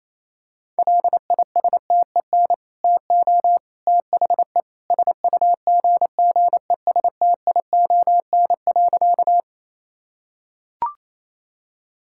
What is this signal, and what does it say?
Morse code, heard as a keyed tone, about 28 wpm, 700 Hz. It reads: LISTEN TO THE SUGGESTION.
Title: Sentences from Top 1600 Words 28wpm